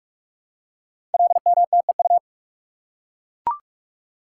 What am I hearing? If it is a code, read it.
Morse code, heard as a keyed tone, about 44 wpm, 690 Hz. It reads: LMTEU
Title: Code Groups - 5 Letters - 45wpm